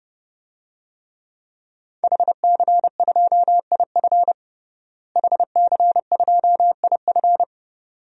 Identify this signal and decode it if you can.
Morse code, heard as a keyed tone, about 30 wpm, 700 Hz. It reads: HC2IF HC2IF
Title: DXCC-100 Call Signs - 5 Characters - Repeated 2x - 30wpm